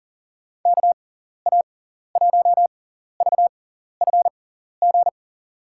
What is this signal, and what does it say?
Morse code, heard as a keyed tone, about 40 wpm, spaced as for 14 wpm, 700 Hz. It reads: KA1VFG
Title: USA Call Signs - 6 Characters - 6x Character Spacing - 40wpm